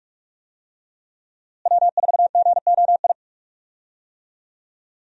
Morse code, heard as a keyed tone, about 43 wpm, 700 Hz. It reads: W4GOI
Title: USA Call Signs - 5 Characters - Repeated 3x - 45wpm